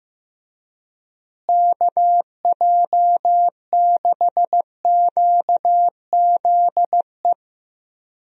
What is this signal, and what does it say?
Morse code, heard as a keyed tone, about 15 wpm, 700 Hz. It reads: KJ6QZE
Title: USA Call Signs - 6 Characters - Repeated 2x - 15wpm